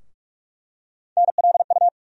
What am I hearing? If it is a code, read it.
Morse code, heard as a keyed tone, about 45 wpm, 700 Hz. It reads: NLU
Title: USA Call Sign Postfixes - 3 Letters 45wpm